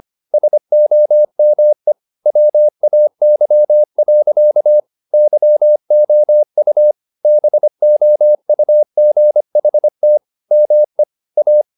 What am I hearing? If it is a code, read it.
Morse code, heard as a keyed tone, about 25 wpm, 600 Hz. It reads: SOME WAY. YOU BOUGHT ME A
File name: the old man and the sea part 1 25wpm